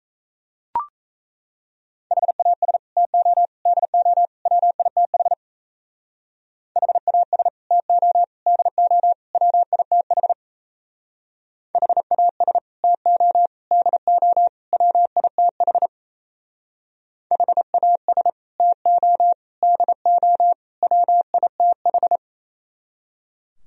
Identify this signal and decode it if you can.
Morse code, heard as a keyed tone, about 36 wpm, 700 Hz. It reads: HAS TO DO WITH HAS TO DO WITH HAS TO DO WITH HAS TO DO WITH